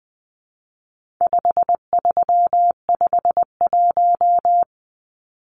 Morse code, heard as a keyed tone, about 20 wpm, 700 Hz. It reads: 5351
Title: Numbers - 4-Digit 20wpm